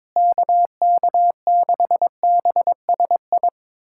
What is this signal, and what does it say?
Morse code, heard as a keyed tone, about 22 wpm, 700 Hz. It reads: KK6BSI
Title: USA Callsign DE Callsign - 6 Characters - 22wpm